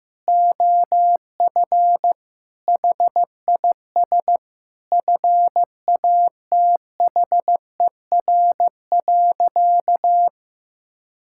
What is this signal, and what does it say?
Morse code, heard as a keyed tone, about 14 wpm, 700 Hz. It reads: OF HIS FATHER.
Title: Sentences from Top 1000 Words - Review Part 5 - 15wpm